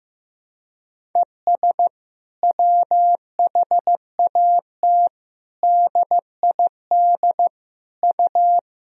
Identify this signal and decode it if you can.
Morse code, heard as a keyed tone, about 15 wpm, 700 Hz. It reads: ES WHAT DID U
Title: Sentences from Top 100 Words and Abbreviations 15wpm